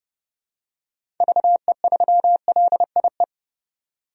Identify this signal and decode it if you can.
Morse code, heard as a keyed tone, about 30 wpm, 700 Hz. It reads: VE3LIE
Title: DXCC-100 Call Signs - 6 Characters - 30wpm